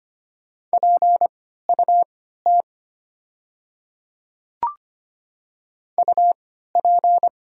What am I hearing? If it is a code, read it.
Morse code, heard as a keyed tone, about 25 wpm, spaced as for 14 wpm, 700 Hz. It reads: PUT UP
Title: 3x Character Spacing - Top 200 Words - 25wpm